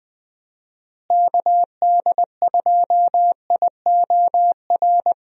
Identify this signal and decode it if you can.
Morse code, heard as a keyed tone, about 20 wpm, 700 Hz. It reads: KD2IOR
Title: USA Call Signs - 6 Characters - Repeated 2x - 20wpm